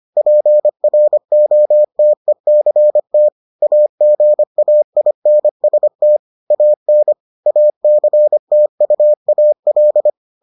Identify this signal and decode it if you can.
Morse code, heard as a keyed tone, about 25 wpm, 600 Hz. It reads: PROTECT AGAINST AN ACTUAL